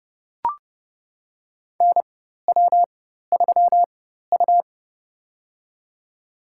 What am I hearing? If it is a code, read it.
Morse code, heard as a keyed tone, about 31 wpm, spaced as for 14 wpm, 700 Hz. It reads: NW3U
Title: USA Call Signs - 4 Characters - 4x Character Spacing - 30wpm